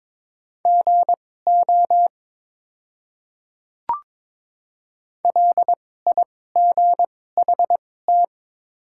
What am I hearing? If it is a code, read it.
Morse code, heard as a keyed tone, about 22 wpm, spaced as for 16 wpm, 700 Hz. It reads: GO LIGHT
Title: 2x Character Spacing - Top 200 Words - 22wpm